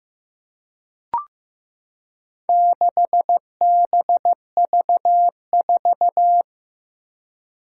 Morse code, heard as a keyed tone, about 15 wpm, 700 Hz. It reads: 6BV4
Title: Four Challenging Letter-Number Combinations 15wpm